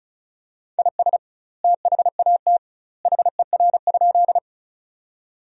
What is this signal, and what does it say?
Morse code, heard as a keyed tone, about 36 wpm, 700 Hz. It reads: IS THAT HER?
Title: 2x Word-Spacing - Sentences from Top 100 Words - 35wpm